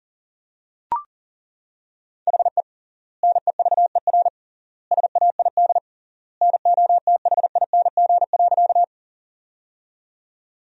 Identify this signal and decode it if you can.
Morse code, heard as a keyed tone, about 37 wpm, 700 Hz. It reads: HE NEVER SAID NOTHING.
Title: 3x Word-Spacing - Sentences from Top 500 Words - 40wpm